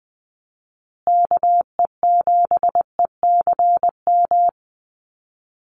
Morse code, heard as a keyed tone, about 20 wpm, 700 Hz. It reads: KE7ECM
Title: USA Call Signs Repeated 2x 20wpm